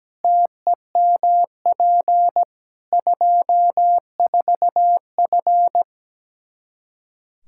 Morse code, heard as a keyed tone, about 17 wpm, 700 Hz. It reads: TEMP 24F
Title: Practice QSO - Exchange 2 - Info - 17wpm